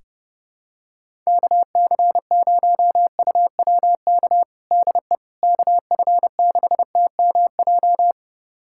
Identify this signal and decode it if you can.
Morse code, heard as a keyed tone, about 32 wpm, 700 Hz. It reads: KC0UWK DE KF6TMJ
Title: USA Callsign DE Callsign - 6 Characters - 30wpm